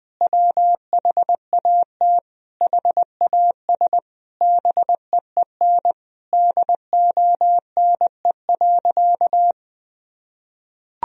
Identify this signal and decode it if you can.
Morse code, heard as a keyed tone, about 20 wpm, 700 Hz. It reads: WHAT HAS BEEN DONE.
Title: Sentences from Top 1000 Words - Review Part 1 - 20wpm